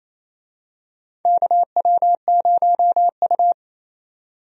Morse code, heard as a keyed tone, about 28 wpm, 700 Hz. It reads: KW0U
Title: USA Call Signs - 4 Characters - 28wpm